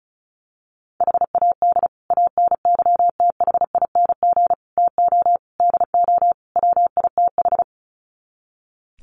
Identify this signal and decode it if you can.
Morse code, heard as a keyed tone, about 34 wpm, 700 Hz. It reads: HAD ANYTHING TO DO WITH
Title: Sets of 5 Words Encore 35wpm